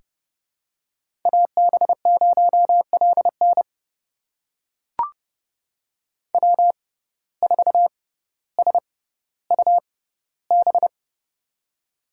Morse code, heard as a keyed tone, about 30 wpm, spaced as for 15 wpm, 700 Hz. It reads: AB0LN W4SUB
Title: USA Call Signs - 5 Characters - 6x Character Spacing - 30wpm